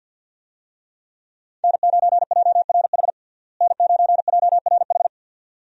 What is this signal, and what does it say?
Morse code, heard as a keyed tone, about 53 wpm, 700 Hz. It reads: N9JRH N9JRH
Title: USA Call Signs - 5 Characters - Repeated 2x - 50wpm